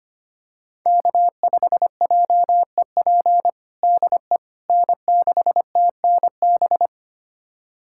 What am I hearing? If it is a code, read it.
Morse code, heard as a keyed tone, about 25 wpm, 700 Hz. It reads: K5JEP DE N6TNB
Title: USA Callsign DE Callsign - 5 Characters - 25wpm